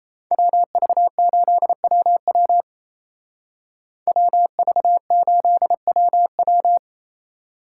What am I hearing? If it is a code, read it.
Morse code, heard as a keyed tone, about 34 wpm, 700 Hz. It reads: WV8WW WV8WW
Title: Speed Racing - USA 2x2 Call Signs 28wpm